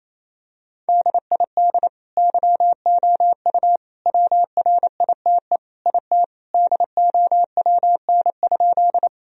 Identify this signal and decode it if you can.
Morse code, heard as a keyed tone, about 27 wpm, 700 Hz. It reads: DID YOU WRITE IT DOWN?
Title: Sentences from Top 1000 Words - Review Part 1 - 28wpm